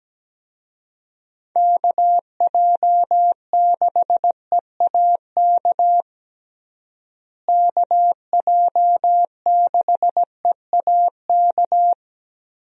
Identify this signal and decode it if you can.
Morse code, heard as a keyed tone, about 17 wpm, 700 Hz. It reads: KJ6EAK KJ6EAK
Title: USA Call Signs - 6 Characters - Repeated 2x - 17wpm